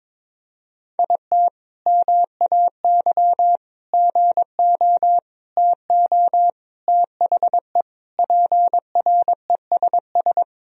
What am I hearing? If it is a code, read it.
Morse code, heard as a keyed tone, about 22 wpm, 700 Hz. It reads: IT MAY GO TO THE PRESS
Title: Sentences from Top 1000 Words - Review Part 1 - 22wpm